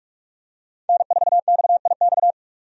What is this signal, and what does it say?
Morse code, heard as a keyed tone, about 46 wpm, 700 Hz. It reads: N4XIX
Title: USA Call Signs - 5 Characters - 45wpm